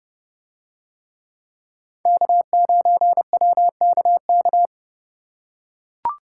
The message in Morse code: K9WKK

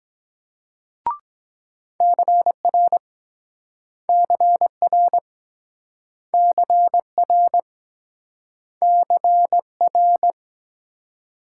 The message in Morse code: CR CR CR CR

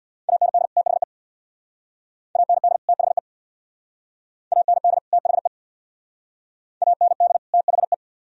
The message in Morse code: AND THE AND THE AND THE AND THE

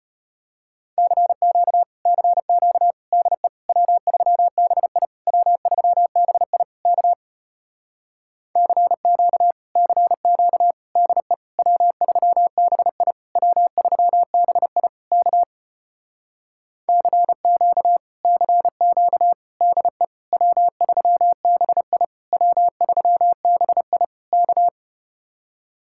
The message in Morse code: CQ CQ DE W3BI W3BI K CQ CQ DE W3BI W3BI K CQ CQ DE W3BI W3BI K

